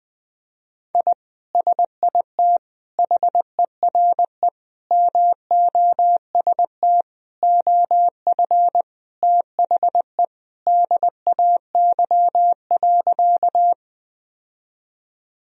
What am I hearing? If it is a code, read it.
Morse code, heard as a keyed tone, about 21 wpm, 700 Hz. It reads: I SIT HERE MOST OF THE DAY.
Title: Sentences from Top 1000 Words - Review Part 1 - 20wpm